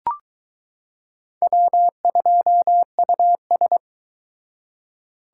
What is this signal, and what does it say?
Morse code, heard as a keyed tone, about 22 wpm, 700 Hz. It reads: W2US